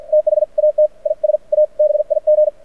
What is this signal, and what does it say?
Morse code, heard as a keyed tone, about 53 wpm, spaced as for 45 wpm, 610 Hz. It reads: THAT IS A BIG
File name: archived / CRSnet